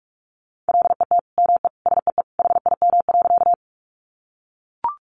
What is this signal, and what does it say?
Morse code, heard as a keyed tone, about 44 wpm, 700 Hz. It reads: LET ME SEE HIM.
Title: Sentences from Top 300 Words 45wpm